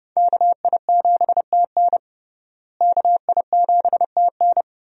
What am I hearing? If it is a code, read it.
Morse code, heard as a keyed tone, about 32 wpm, 700 Hz. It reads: KI7TN KI7TN
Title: USA Call Signs - 5 Characters - Repeated 2x - 30wpm